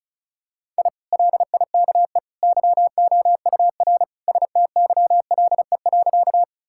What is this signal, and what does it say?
Morse code, heard as a keyed tone, about 36 wpm, 700 Hz. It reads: I LIKE YOUR STYLE.
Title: Sentences from Top 1000 Words - Review Part 5 - 35wpm